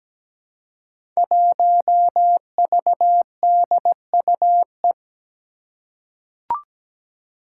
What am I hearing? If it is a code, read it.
Morse code, heard as a keyed tone, about 17 wpm, 700 Hz. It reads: 1VDUE
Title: Five Challenging Letter-Number Combinations 17wpm